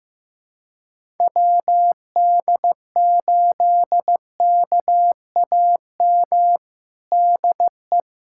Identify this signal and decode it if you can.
Morse code, heard as a keyed tone, about 15 wpm, 700 Hz. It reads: WD8KAM DE
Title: USA Callsign DE Callsign 15wpm